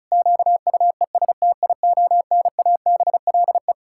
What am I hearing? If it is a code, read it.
Morse code, heard as a keyed tone, about 34 wpm, 700 Hz. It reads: QUESTIONABLE
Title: Pangram Challenge v2 - 35wpm